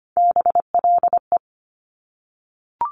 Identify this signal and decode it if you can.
Morse code, heard as a keyed tone, about 25 wpm, 700 Hz. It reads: BLE
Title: Top Three Letter Combinations in Words 25wpm